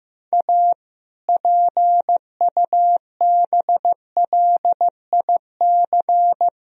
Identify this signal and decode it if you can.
Morse code, heard as a keyed tone, about 15 wpm, 700 Hz. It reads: A PUBLIC
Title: Sentences from Top 1000 Words - Review Part 3 - 15wpm